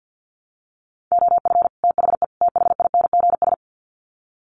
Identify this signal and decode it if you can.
Morse code, heard as a keyed tone, about 50 wpm, 700 Hz. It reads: OF THE THINGS
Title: Sets of 3 Words 50wpm